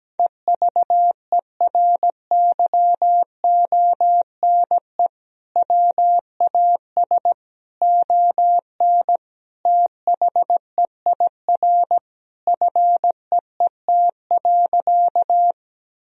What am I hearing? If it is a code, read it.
Morse code, heard as a keyed tone, about 17 wpm, 700 Hz. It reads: EVERYONE WAS ON THEIR FEET.
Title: Sentences from Top 1000 Words - Review Part 1 - 17wpm